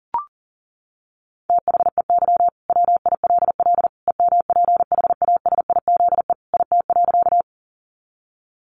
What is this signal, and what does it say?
Morse code, heard as a keyed tone, about 40 wpm, 700 Hz. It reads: THEY WILL EMPHASIZE IT.
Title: Sentences from Top 1500 Words 40wpm